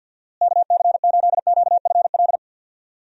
Morse code, heard as a keyed tone, about 52 wpm, 700 Hz. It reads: KC8QFL
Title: USA Call Signs - 6 Characters - Repeated 3x - 50wpm